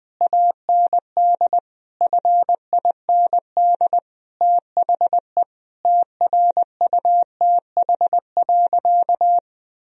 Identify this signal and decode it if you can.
Morse code, heard as a keyed tone, about 20 wpm, 700 Hz. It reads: AND FIND THE TRUTH.
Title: Sentences from Top 1000 Words - Review Part 3 - 20wpm